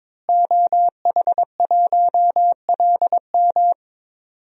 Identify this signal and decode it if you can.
Morse code, heard as a keyed tone, about 22 wpm, 700 Hz. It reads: OH1LM
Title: DXCC-100 Call Signs - 5 Characters - Repeated 2x - 22wpm